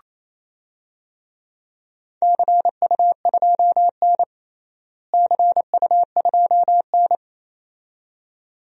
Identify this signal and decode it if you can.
Morse code, heard as a keyed tone, about 27 wpm, 700 Hz. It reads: CU2N CU2N
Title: DXCC-100 Call Signs - 4 Characters - Repeated 2x - 28wpm